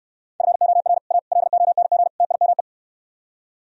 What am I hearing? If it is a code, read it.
Morse code, heard as a keyed tone, about 79 wpm, 700 Hz. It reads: FOR A LONG TIME